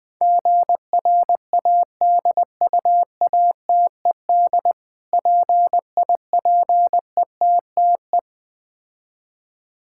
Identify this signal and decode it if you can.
Morse code, heard as a keyed tone, about 20 wpm, 700 Hz. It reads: GRADUATED PIPETTE
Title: Difficult Words - Speed-Racing with Hints - 20wpm